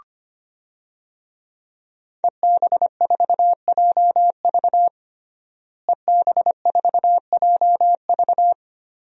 Morse code, heard as a keyed tone, about 25 wpm, 700 Hz. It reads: EB4JV EB4JV
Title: DXCC-100 Call Signs - 5 Characters - Repeated 3x - 25wpm